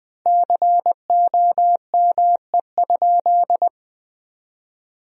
Morse code, heard as a keyed tone, about 20 wpm, 700 Hz. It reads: COME?
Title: 2x Word-Spacing - Sentences from Top 200 Words - 20wpm